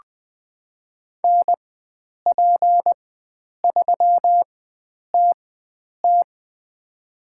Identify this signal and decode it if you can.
Morse code, heard as a keyed tone, about 20 wpm, spaced as for 9 wpm, 700 Hz. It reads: NP3TT
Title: USA Call Signs - 5 Characters - 4x Character Spacing - 20wpm